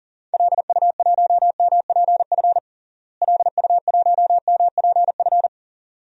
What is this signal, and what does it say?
Morse code, heard as a keyed tone, about 40 wpm, 700 Hz. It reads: LU1MPF LU1MPF
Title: DXCC-100 Call Signs - 6 Characters - Repeated 2x - 40wpm